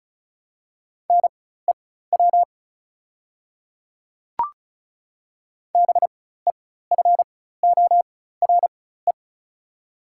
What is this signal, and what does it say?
Morse code, heard as a keyed tone, about 36 wpm, spaced as for 16 wpm, 700 Hz. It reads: NEW BEFORE
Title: 4x Character Spacing - Top 200 Words - 35wpm